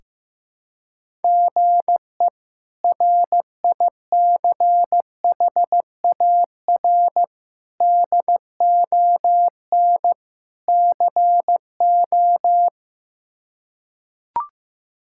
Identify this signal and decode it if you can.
Morse code, heard as a keyed tone, about 15 wpm, 700 Hz. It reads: GE RICHAR DON CO
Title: K1USN SST QSO Practice - 15wpm